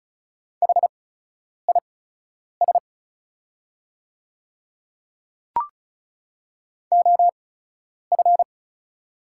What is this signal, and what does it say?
Morse code, heard as a keyed tone, about 36 wpm, spaced as for 10 wpm, 690 Hz. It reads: HIS OF
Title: 8x Character Spacing - Top 200 Words - 35wpm